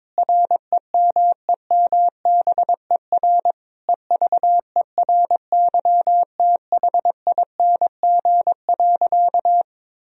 Morse code, heard as a keyed tone, about 22 wpm, 700 Hz. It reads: REMEMBER EVERYTHING.
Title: Sentences from Top 1000 Words - Review Part 1 - 22wpm